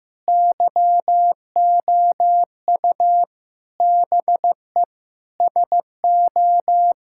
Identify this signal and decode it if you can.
Morse code, heard as a keyed tone, about 16 wpm, 700 Hz. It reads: YOU BE SO
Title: Sentences from Top 1000 Words - Review Part 6 - 15wpm